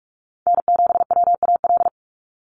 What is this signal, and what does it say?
Morse code, heard as a keyed tone, about 46 wpm, 690 Hz. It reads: N7WAL
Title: USA Call Signs 45wpm